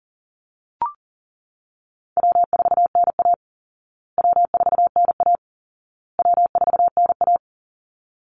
Morse code, heard as a keyed tone, about 40 wpm, 700 Hz. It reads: W4NA W4NA W4NA
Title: Easy Call Signs Repeated 3x - 40wpm